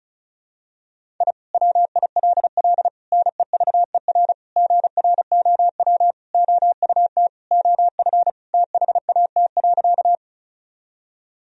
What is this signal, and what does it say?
Morse code, heard as a keyed tone, about 34 wpm, 700 Hz. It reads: I WILL NEVER GROW OUT OF THAT.
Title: Sentences from Top 1000 Words - Review Part 1 - 35wpm